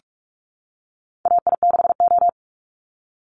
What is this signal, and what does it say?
Morse code, heard as a keyed tone, about 44 wpm, 700 Hz. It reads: AI6O
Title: Call Signs Repeated 3x - 45wpm